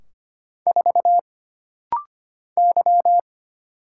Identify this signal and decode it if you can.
Morse code, heard as a keyed tone, about 25 wpm, 700 Hz. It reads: 4 Y
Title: Single Letter-Number Alt - Rapid-Fire 25wpm